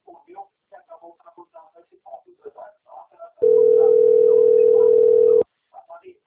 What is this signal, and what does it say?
Ringback tone (North America), 440 + 480 Hz, with speech.
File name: out- - - - - .9836